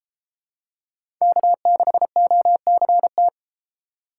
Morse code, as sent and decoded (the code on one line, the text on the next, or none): -.- -.... --- -.-. -
K6OCT